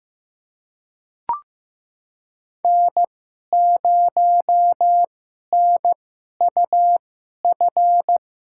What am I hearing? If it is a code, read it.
Morse code, heard as a keyed tone, about 15 wpm, spaced as for 11 wpm, 700 Hz. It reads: N0NUF
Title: USA Call Signs - 5 Characters - 2x Character Spacing - 15wpm